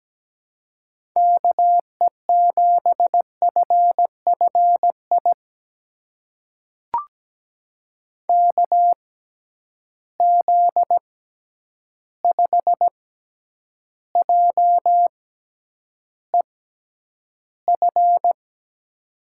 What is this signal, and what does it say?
Morse code, heard as a keyed tone, about 17 wpm, spaced as for 9 wpm, 700 Hz. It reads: KE7FFI KZ5JEF
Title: USA Call Signs - 6x Character Spacing - 17wpm